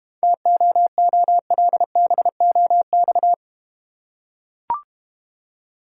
Morse code, heard as a keyed tone, about 30 wpm, 700 Hz. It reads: TOOLBOX
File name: practice / 25wpm / Difficult Words - Speed-Racing with Hints - 25wpm